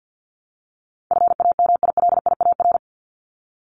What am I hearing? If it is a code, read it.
Morse code, heard as a keyed tone, about 50 wpm, 700 Hz. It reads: FAMILIAR
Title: Top 751-1000 Words 50wpm